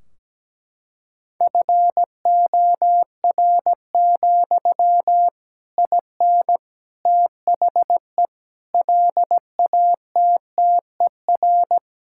Morse code, heard as a keyed tone, about 17 wpm, 700 Hz. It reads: FOR, IN THE LATTER
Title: Sentences from Top 1600 Words 17wpm